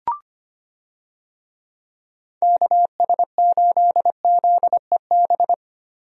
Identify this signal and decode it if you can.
Morse code, heard as a keyed tone, about 25 wpm, 700 Hz. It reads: KS8ZEB
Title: USA Call Signs - 6 Characters - Repeated 3x - 25wpm